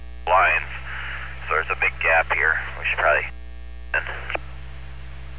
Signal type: Emergency Alert System (SAME) tones